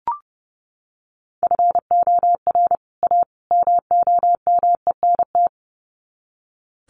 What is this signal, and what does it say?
Morse code, heard as a keyed tone, about 30 wpm, 700 Hz. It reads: FOR A MOMENT
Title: Sets of 3 Words 30wpm